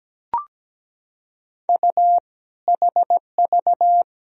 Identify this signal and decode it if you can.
Morse code, heard as a keyed tone, about 17 wpm, 700 Hz. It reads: U HV
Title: Sentences from Top 100 Words and Abbreviations 17wpm